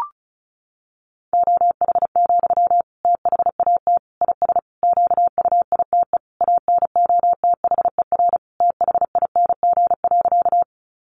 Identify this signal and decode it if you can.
Morse code, heard as a keyed tone, about 34 wpm, 700 Hz. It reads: OH, THAT IS QUITE ANOTHER THING.
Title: Sentences from Top 700 Words 35wpm